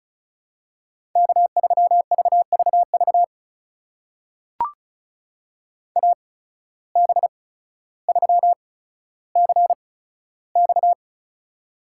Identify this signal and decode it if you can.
Morse code, heard as a keyed tone, about 36 wpm, spaced as for 15 wpm, 700 Hz. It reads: K3VVV AB3CX
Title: USA Call Signs - 5 Characters - 8x Character Spacing - 35wpm